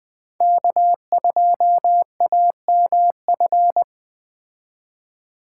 Morse code, heard as a keyed tone, about 20 wpm, 700 Hz. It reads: K2AMF
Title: Speed Racing - USA 1x3 Call Signs 17wpm